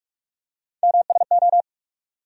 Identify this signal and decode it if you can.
Morse code, heard as a keyed tone, about 46 wpm, 700 Hz. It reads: MSO
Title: USA Call Sign Postfixes - 3 Letters 45wpm